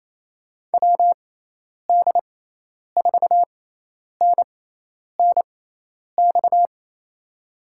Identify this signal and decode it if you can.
Morse code, heard as a keyed tone, about 28 wpm, spaced as for 10 wpm, 700 Hz. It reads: WD4NNX
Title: USA Call Signs - 6x Character Spacing - 28wpm